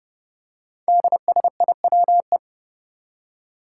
Morse code, heard as a keyed tone, about 29 wpm, 700 Hz. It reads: DSIWE
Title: Five Challenging Letter-Number Combinations 30wpm